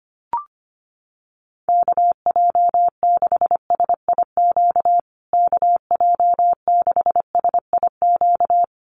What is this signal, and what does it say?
Morse code, heard as a keyed tone, about 25 wpm, 700 Hz. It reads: KJ6SIQ KJ6SIQ
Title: USA Call Signs Repeated 2x 25wpm